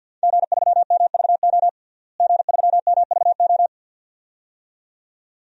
Morse code, heard as a keyed tone, about 51 wpm, 700 Hz. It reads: G3MVO G3MVO
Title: DXCC-100 Call Signs - 5 Characters - Repeated 2x - 50wpm